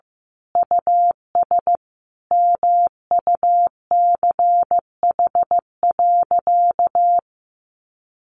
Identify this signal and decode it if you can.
Morse code, heard as a keyed tone, about 15 wpm, 700 Hz. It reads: US MUCH.
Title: Sentences from Top 1500 Words 15wpm